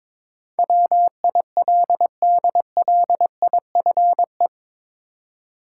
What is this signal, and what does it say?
Morse code, heard as a keyed tone, about 22 wpm, 700 Hz. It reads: WILDLIFE
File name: practice / 22wpm / Eight Letter Words - 22wpm